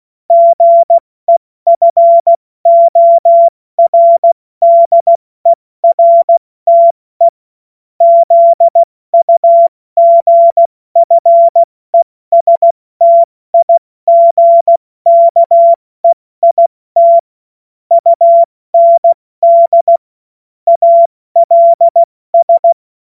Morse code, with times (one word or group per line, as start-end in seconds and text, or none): 0.3-7.3 GEFORDERTE
8.0-17.2 ZUGFESTIGKEIT
17.9-20.0 UND
20.7-22.7 ALS